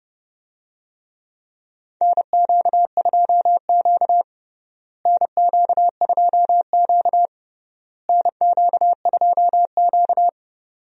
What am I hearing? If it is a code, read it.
Morse code, heard as a keyed tone, about 30 wpm, 700 Hz. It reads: NQ2Q NQ2Q NQ2Q
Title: USA Call Signs - 4 Characters - Repeated 3x - 30wpm